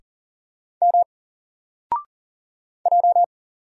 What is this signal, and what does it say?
Morse code, heard as a keyed tone, about 40 wpm, 700 Hz. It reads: M J
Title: Single Letter-Number 40wpm